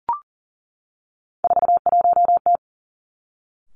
Morse code, heard as a keyed tone, about 40 wpm, 700 Hz. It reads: 41T
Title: Three Challenging Letter-Number Combinations 40wpm